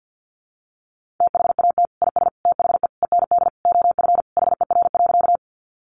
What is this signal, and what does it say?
Morse code, heard as a keyed tone, about 50 wpm, 700 Hz. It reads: THAT IS THE END OF HER.
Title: Sentences from Top 200 Words 50wpm